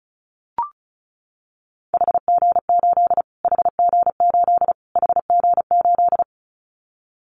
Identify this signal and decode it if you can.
Morse code, heard as a keyed tone, about 34 wpm, 700 Hz. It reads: HG8 HG8 HG8
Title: Callsign Prefixes for Top 25 DXCC - Repeated 3x - 35wpm